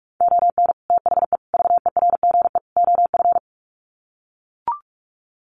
Morse code, heard as a keyed tone, about 44 wpm, 700 Hz. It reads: ON THE VERGE OF
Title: Sets of 4 Words 45wpm